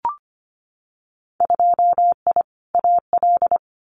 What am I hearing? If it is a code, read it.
Morse code, heard as a keyed tone, about 25 wpm, 700 Hz. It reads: 2I AL
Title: Winter Field Day Exchanges 25wpm